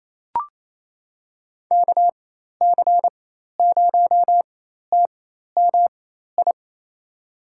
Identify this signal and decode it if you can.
Morse code, heard as a keyed tone, about 28 wpm, spaced as for 13 wpm, 700 Hz. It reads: KC0TMI
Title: USA Call Signs - 6 Characters - 4x Character Spacing - 28wpm